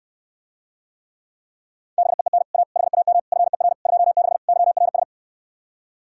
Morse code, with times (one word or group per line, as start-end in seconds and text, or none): 2.0-2.4 BEEN
2.5-2.6 A
2.8-3.2 HAM
3.3-3.7 FER
3.9-4.4 26
4.5-5.0 YRS